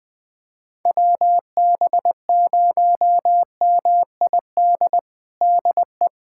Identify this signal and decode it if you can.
Morse code, heard as a keyed tone, about 20 wpm, 700 Hz. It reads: WB0MID DE
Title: USA Callsign DE Callsign - 6 Characters - 20wpm